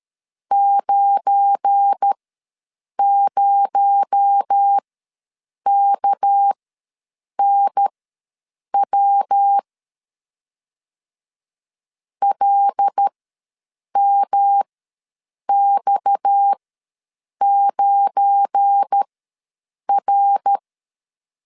Morse code, heard as a keyed tone, about 13 wpm, spaced as for 7 wpm, 800 Hz. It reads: 90KNW LMX9R
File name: LEZIONI / 7 GF9Z / 7 GF9 ws 12wpm 3344